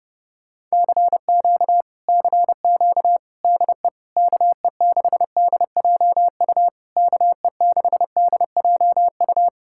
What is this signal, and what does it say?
Morse code, heard as a keyed tone, about 33 wpm, 700 Hz. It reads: CQ CQ DE KE6DJU KE6DJU